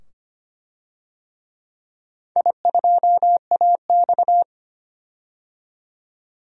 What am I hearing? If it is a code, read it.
Morse code, heard as a keyed tone, about 25 wpm, 700 Hz. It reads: I2AX
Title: DXCC-100 Call Signs - 4 Characters - 25wpm